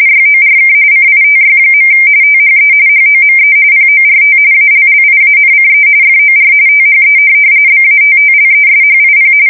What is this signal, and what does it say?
RTTY, about 45.45 baud. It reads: YORK CITY. THE HOME PORT ALLIANCE FOR THE BATTLESHIP
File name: fsk170 / USS-New-Jersey